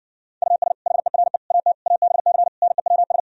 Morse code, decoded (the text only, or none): US HERE AT ABC NEWS